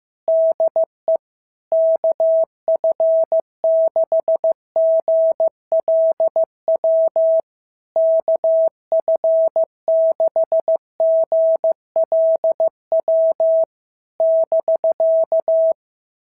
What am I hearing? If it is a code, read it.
Morse code, heard as a keyed tone, about 15 wpm, 690 Hz. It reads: DE KF6GLW KF6GLW <BK>
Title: SOTA QSO Practice 15wpm